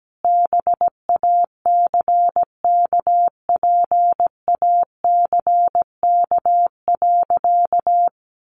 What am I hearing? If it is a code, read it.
Morse code, heard as a keyed tone, about 17 wpm, 700 Hz. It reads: BACKPACK.
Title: Additional Easy Phrases 17wpm